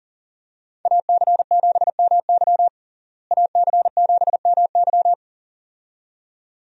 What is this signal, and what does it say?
Morse code, heard as a keyed tone, about 37 wpm, 700 Hz. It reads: AC7MY AC7MY
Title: USA Call Signs - 5 Characters - Repeated 2x - 40wpm